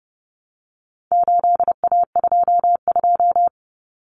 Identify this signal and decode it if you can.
Morse code, heard as a keyed tone, about 30 wpm, 700 Hz. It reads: 8A22
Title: Common Cut Numbers - 4-Digit 30wpm